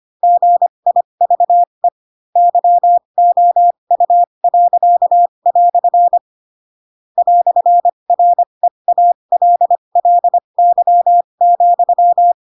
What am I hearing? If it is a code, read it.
Morse code, heard as a keyed tone, about 25 wpm, spaced as for 22 wpm, 700 Hz. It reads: GIVE YOU." "REALLY,
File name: Pinocchio / Pinocchio ch7 18 25 ik1hge